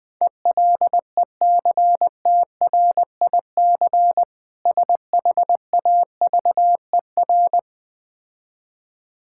Morse code, heard as a keyed tone, about 20 wpm, 700 Hz. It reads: ELECTRIC SHAVER
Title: Difficult Words - Speed-Racing with Hints - 20wpm